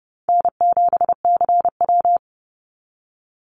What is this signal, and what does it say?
Morse code, heard as a keyed tone, about 30 wpm, 700 Hz. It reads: N7CW
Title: NA SOTA Activator Callsigns 30wpm